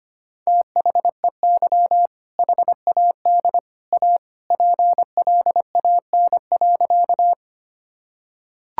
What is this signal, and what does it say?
Morse code, heard as a keyed tone, about 25 wpm, 700 Hz. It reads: THEY HAD A PLAN.
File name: practice / 25wpm / Sentences with 2-Word Phrases - 25wpm